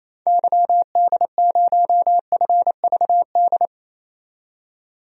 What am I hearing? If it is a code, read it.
Morse code, heard as a keyed tone, about 28 wpm, 700 Hz. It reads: YD0FVD
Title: DXCC-100 Call Signs - 6 Characters - 28wpm